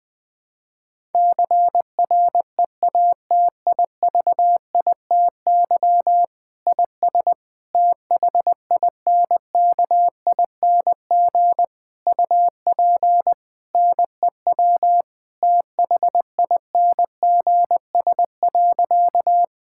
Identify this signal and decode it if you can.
Morse code, heard as a keyed tone, about 20 wpm, 700 Hz. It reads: CREATIVITY IS THINKING UP NEW THINGS.
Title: Difficult Sentences with Hints - 20wpm